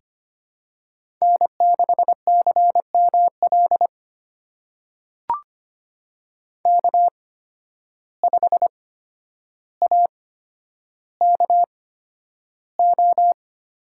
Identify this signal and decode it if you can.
Morse code, heard as a keyed tone, about 25 wpm, spaced as for 11 wpm, 700 Hz. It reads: N6CML K5AKO
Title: USA Call Signs - 8x Character Spacing - 25wpm